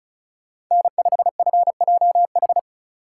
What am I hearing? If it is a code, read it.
Morse code, heard as a keyed tone, about 44 wpm, 700 Hz. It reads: N5FJH
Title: Speed Racing - USA 1x3 Call Signs 35wpm